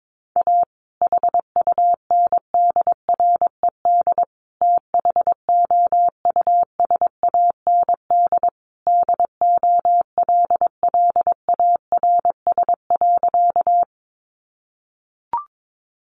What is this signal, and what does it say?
Morse code, heard as a keyed tone, about 22 wpm, 700 Hz. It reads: A HUNDRED THOUSAND DOLLARS.
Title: A Sound Of Thunder 22wpm